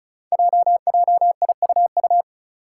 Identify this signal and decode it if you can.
Morse code, heard as a keyed tone, about 34 wpm, 700 Hz. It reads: JJIUU
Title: Five Challenging Letter-Number Combinations 35wpm